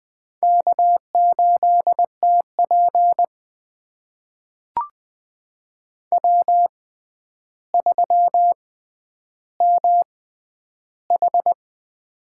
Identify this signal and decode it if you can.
Morse code, heard as a keyed tone, about 21 wpm, spaced as for 10 wpm, 700 Hz. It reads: K8TP W3MH
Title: USA Call Signs - 4 Characters - 6x Character Spacing - 20wpm